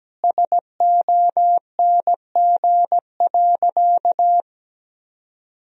Morse code, heard as a keyed tone, about 17 wpm, 700 Hz. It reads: SONG.
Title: Sentences from Top 1000 Words - Review Part 2 - 17wpm